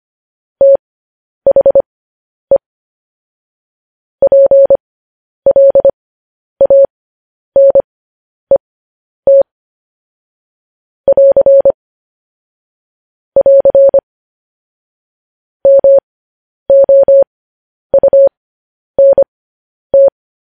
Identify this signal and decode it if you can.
Morse code, heard as a keyed tone, about 25 wpm, spaced as for 10 wpm, 550 Hz. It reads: THE PLANET <AR> <AR> MOUNT
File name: podcast / News headlines-1x-25x10